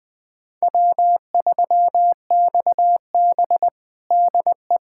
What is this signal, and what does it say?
Morse code, heard as a keyed tone, about 21 wpm, 700 Hz. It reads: W3XB DE
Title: USA Callsign DE Callsign - 4 Characters - 20wpm